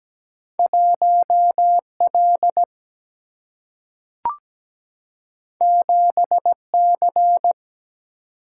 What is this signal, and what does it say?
Morse code, heard as a keyed tone, about 17 wpm, 700 Hz. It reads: 1L 7C
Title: Single-Letter Single-Number 17wpm